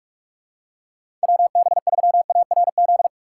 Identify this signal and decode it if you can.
Morse code, heard as a keyed tone, about 46 wpm, 700 Hz. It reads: WB3ARZ
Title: USA Call Signs - 6 Characters - 3x Character Spacing - 45wpm